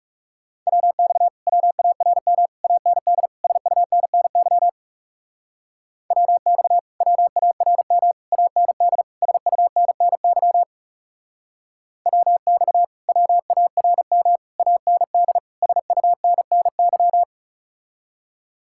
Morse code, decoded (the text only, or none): WX WARM AND SUNNY WX WARM AND SUNNY WX WARM AND SUNNY